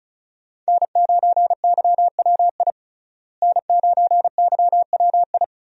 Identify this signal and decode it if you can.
Morse code, heard as a keyed tone, about 36 wpm, 700 Hz. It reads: N9YWI N9YWI
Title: USA Call Signs - 5 Characters - Repeated 3x - 35wpm